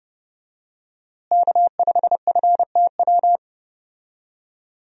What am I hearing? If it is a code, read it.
Morse code, heard as a keyed tone, about 30 wpm, 700 Hz. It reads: K5FTW
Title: USA Call Signs - 5 Characters - Repeated 3x - 30wpm